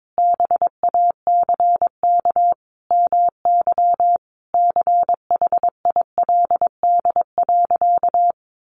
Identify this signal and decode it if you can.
Morse code, heard as a keyed tone, about 22 wpm, 700 Hz. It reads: BACK MY CHILD.
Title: Sentences from Top 800 Words 22wpm